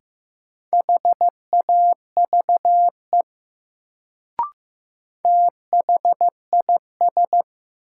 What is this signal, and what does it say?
Morse code, heard as a keyed tone, about 15 wpm, 700 Hz. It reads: HAVE THIS
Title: Sentences from Top 100 Words - New Word Drill - 15wpm